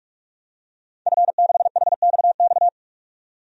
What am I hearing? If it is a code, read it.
Morse code, heard as a keyed tone, about 46 wpm, 700 Hz. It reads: F6HXX
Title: DXCC-100 Call Signs - 5 Characters - 45wpm